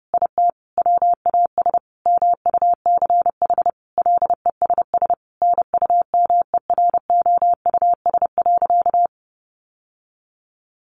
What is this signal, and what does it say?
Morse code, heard as a keyed tone, about 30 wpm, 700 Hz. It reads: IT WAS MUCH LESS NUMEROUS.
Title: Sentences from Top 1400 Words 30wpm